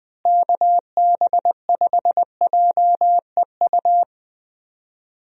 Morse code, heard as a keyed tone, about 20 wpm, 700 Hz. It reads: KB5JEU
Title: USA Call Signs - 6 Characters - Repeated 2x - 20wpm